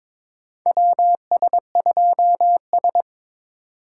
Morse code, heard as a keyed tone, about 22 wpm, 700 Hz. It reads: WS2S
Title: USA Call Signs - 4 Characters - 3x Character Spacing - 22wpm